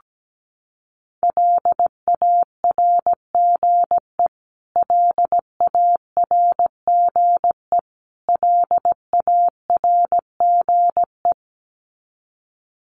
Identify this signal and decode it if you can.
Morse code, heard as a keyed tone, about 17 wpm, 700 Hz. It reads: LARGE LARGE LARGE
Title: Top 300 Words Repeated 17wpm